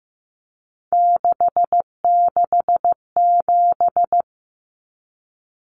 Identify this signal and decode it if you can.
Morse code, heard as a keyed tone, about 15 wpm, 700 Hz. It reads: 667
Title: Common Cut Numbers - 3-Digit 15wpm